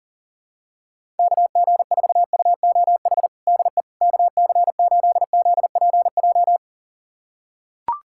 Morse code, heard as a keyed tone, about 40 wpm, 700 Hz. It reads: KG4UOH DE KC8ZPJ
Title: USA Callsign DE Callsign - 6 Characters - 40wpm